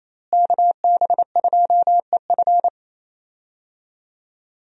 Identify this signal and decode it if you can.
Morse code, heard as a keyed tone, about 28 wpm, 700 Hz. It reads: KB2EF
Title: USA Call Signs - 5 Characters - Repeated 2x - 28wpm